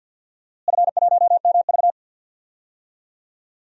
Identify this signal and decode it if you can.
Morse code, heard as a keyed tone, about 50 wpm, 700 Hz. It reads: F1MV
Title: DXCC-100 Call Signs - 4 Characters - Repeated 2x - 50wpm